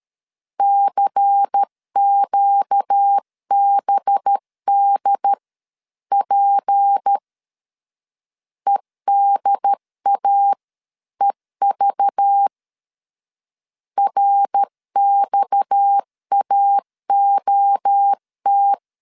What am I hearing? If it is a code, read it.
Morse code, heard as a keyed tone, about 13 wpm, spaced as for 11 wpm, 800 Hz. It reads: CQBDP EDAEV RXAOT
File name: CW / DET / 1 6 1500C S C 12wpm 3533